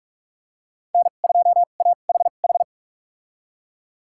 Morse code, heard as a keyed tone, about 47 wpm, spaced as for 33 wpm, 700 Hz. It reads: N2AHH
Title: USA Call Signs - 5 Characters - 2x Character Spacing - 45wpm